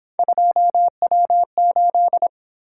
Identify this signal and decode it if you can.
Morse code, heard as a keyed tone, about 26 wpm, 700 Hz. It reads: 2W8
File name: Speed-Racing - Three Letter-Number Combinations - 22wpm